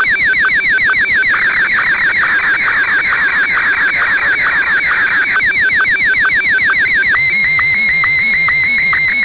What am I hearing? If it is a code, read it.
SSTV (slow-scan television), Martin 1.